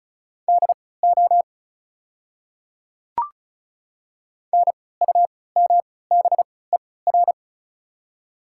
Morse code, heard as a keyed tone, about 36 wpm, spaced as for 20 wpm, 690 Hz. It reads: DO NUMBER
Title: 3x Character Spacing - Top 100 Words - 35wpm